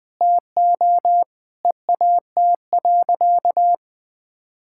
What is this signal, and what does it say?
Morse code, heard as a keyed tone, about 20 wpm, 700 Hz. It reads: TO EAT.
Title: Sentences from Top 1000 Words - Review Part 5 - 20wpm